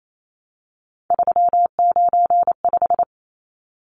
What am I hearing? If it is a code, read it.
Morse code, heard as a keyed tone, about 28 wpm, 700 Hz. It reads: 395
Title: Numbers - 3-Digit 28wpm